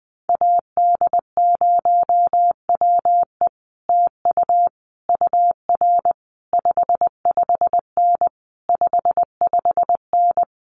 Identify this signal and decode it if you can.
Morse code, heard as a keyed tone, about 19 wpm, 700 Hz. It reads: AD0WE TU UR 55N 55N
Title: Common QSO Phrases Encore 20wpm